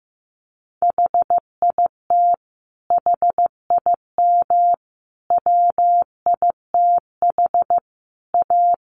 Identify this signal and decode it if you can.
Morse code, heard as a keyed tone, about 15 wpm, 700 Hz. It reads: HIT HIM WITH A
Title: Sentences from Top 1200 Words 15wpm